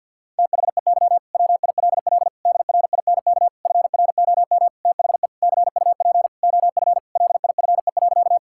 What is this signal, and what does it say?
Morse code, heard as a keyed tone, about 50 wpm, 700 Hz. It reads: THEY WILL DRINK FROM THE CUP OF LIFE.
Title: Sentences from Top 1600 Words 50wpm